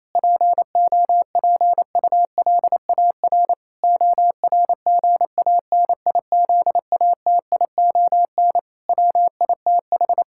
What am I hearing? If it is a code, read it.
Morse code, heard as a keyed tone, about 28 wpm, 700 Hz. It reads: POPULAR ORGANIZATION WITH
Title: Sentences from Top 1000 Words - Review Part 5 - 28wpm